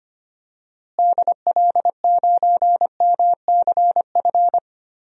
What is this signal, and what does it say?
Morse code, heard as a keyed tone, about 25 wpm, 700 Hz. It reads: DL9MCF
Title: DXCC-100 Call Signs - 6 Characters - Repeated 3x - 25wpm